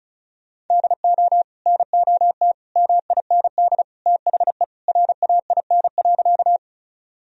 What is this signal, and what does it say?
Morse code, heard as a keyed tone, about 34 wpm, 700 Hz. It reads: DO NOT MIND THE RAIN.
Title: Sentences from Top 1000 Words - Review Part 1 - 35wpm